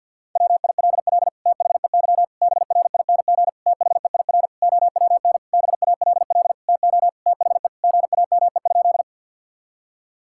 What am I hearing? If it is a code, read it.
Morse code, heard as a keyed tone, about 50 wpm, 700 Hz. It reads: WILL THEY BRING THEIR OWN BALL TO THE GAME?
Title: Sentences from Top 1000 Words - Review Part 6 - 50wpm